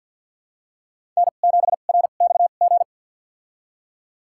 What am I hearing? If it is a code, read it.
Morse code, heard as a keyed tone, about 50 wpm, spaced as for 36 wpm, 700 Hz. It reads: N7RXG
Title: USA Call Signs - 5 Characters - 2x Character Spacing - 50wpm